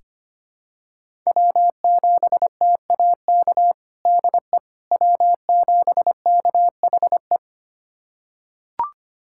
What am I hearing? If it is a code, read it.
Morse code, heard as a keyed tone, about 25 wpm, 700 Hz. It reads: W7TAK DE W7KHE
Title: USA Callsign DE Callsign - 5 Characters - 25wpm